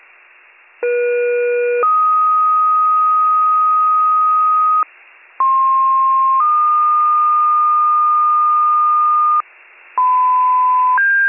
Two-tone page, heard as a one-second tone, then a three-second tone, about 480 then 1250 Hz, about 1060 then 1220 Hz. It is heard from 0.8 to 9.4 s.